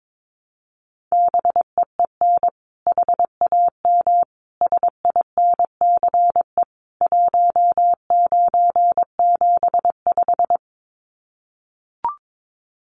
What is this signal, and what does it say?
Morse code, heard as a keyed tone, about 22 wpm, 700 Hz. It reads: BEEN HAM SINCE 1975
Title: Common QSO Phrases 22wpm